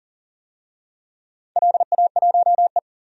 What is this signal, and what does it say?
Morse code, heard as a keyed tone, about 41 wpm, 700 Hz. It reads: LA1E
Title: DXCC-100 Call Signs - 4 Characters - 40wpm